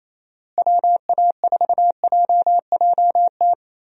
Morse code, heard as a keyed tone, about 28 wpm, 700 Hz. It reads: WA4JJT
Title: USA Call Signs - 6 Characters - Repeated 2x - 28wpm